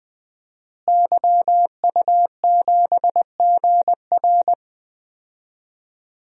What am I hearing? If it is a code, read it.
Morse code, heard as a keyed tone, about 20 wpm, 700 Hz. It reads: YU7GR
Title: DXCC-100 Call Signs - 5 Characters - Repeated 2x - 20wpm